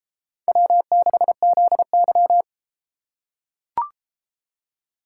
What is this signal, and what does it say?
Morse code, heard as a keyed tone, about 30 wpm, 700 Hz. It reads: W6ZY